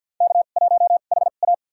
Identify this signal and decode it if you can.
Morse code, heard as a keyed tone, about 52 wpm, spaced as for 37 wpm, 700 Hz. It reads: K1HA
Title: USA Call Signs - 4 Characters - 2x Character Spacing - 50wpm